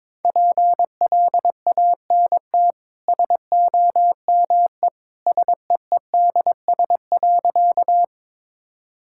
Morse code, heard as a keyed tone, about 22 wpm, 690 Hz. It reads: PLANT SOME SEEDS.
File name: practice / 22wpm / Easy Sentences with Hints - 22wpm